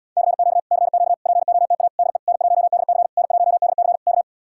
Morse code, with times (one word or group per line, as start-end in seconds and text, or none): 0.2-0.6 CQ
0.7-1.1 CQ
1.3-1.9 POTA
2.0-2.2 DE
2.3-3.1 N1DC
3.2-4.0 N1DC
4.1-4.2 K